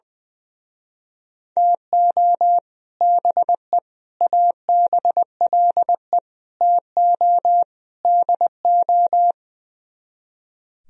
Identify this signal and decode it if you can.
Morse code, heard as a keyed tone, about 20 wpm, 700 Hz. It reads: TO BE ABLE TO DO
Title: Speed-Racing - Sets of 5 Words 20wpm